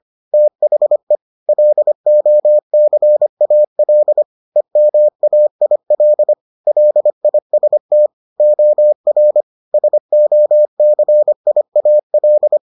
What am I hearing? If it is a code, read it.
Morse code, heard as a keyed tone, about 25 wpm, 600 Hz. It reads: THE LOCAL EMAIL LIST OR SOCIAL